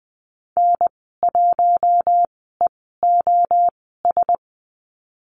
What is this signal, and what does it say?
Morse code, heard as a keyed tone, about 20 wpm, spaced as for 14 wpm, 700 Hz. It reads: N1EOS
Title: 076 Lesson 13g - Letter L - Call Signs 20wpm